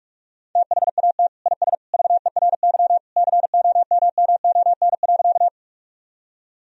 Morse code, heard as a keyed tone, about 42 wpm, 700 Hz. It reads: THAT IS VERY COMMON.
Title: Sentences from Top 1000 Words - Review Part 3 - 45wpm